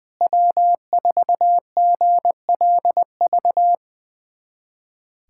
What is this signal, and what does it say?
Morse code, heard as a keyed tone, about 20 wpm, 700 Hz. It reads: W4GLV
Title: USA Callsign DE Callsign - 5 Characters - 20wpm